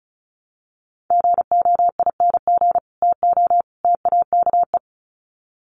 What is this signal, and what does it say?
Morse code, heard as a keyed tone, about 34 wpm, 700 Hz. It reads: GOING TO TAKE
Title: Sets of 3 Words 35wpm